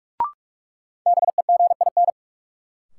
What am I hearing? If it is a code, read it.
Morse code, heard as a keyed tone, about 46 wpm, 700 Hz. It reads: BEGIN
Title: Sentences from Top 300 Words - New Word Drill - 45wpm